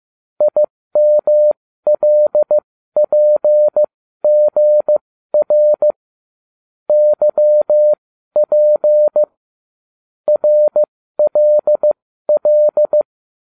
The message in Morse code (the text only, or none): IMLPGR YP RLL